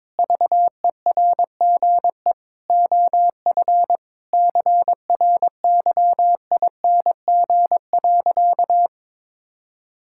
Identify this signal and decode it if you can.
Morse code, heard as a keyed tone, about 22 wpm, 700 Hz. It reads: VERGE OF CRYING.
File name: practice / 22wpm / Sentences with 4-Word Phrases - 22wpm